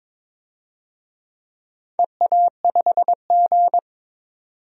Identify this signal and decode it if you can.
Morse code, heard as a keyed tone, about 21 wpm, 700 Hz. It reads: EA5G
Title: DXCC-100 Call Signs - 4 Characters - 22wpm